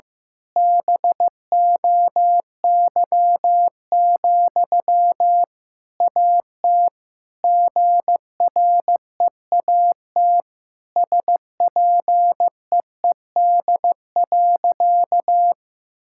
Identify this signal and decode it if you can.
Morse code, heard as a keyed tone, about 16 wpm, 700 Hz. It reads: BOY, AT GREAT SPEED.
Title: Sentences from Top 1000 Words - Review Part 5 - 15wpm